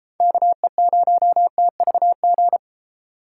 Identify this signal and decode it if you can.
Morse code, heard as a keyed tone, about 31 wpm, 700 Hz. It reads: KE0TVG